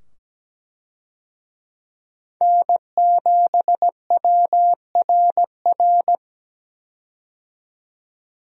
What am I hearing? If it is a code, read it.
Morse code, heard as a keyed tone, about 17 wpm, 700 Hz. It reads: N7WRR
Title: USA Call Signs - 5 Characters - 17wpm